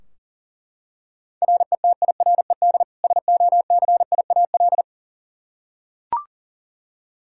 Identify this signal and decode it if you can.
Morse code, heard as a keyed tone, about 40 wpm, 700 Hz. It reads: RETIRED SOCIAL
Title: Practice QSO - Exchange 3 - More Info - 40wpm